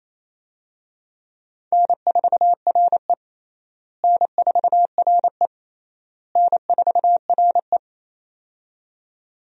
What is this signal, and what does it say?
Morse code, heard as a keyed tone, about 28 wpm, 700 Hz. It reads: N4RE N4RE N4RE
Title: USA Call Signs - 4 Characters - Repeated 3x - 28wpm